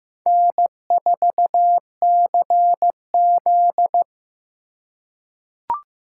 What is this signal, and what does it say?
Morse code, heard as a keyed tone, about 15 wpm, 700 Hz. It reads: N4CZ
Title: USA Call Signs - 4 Characters - 2x Character Spacing - 15wpm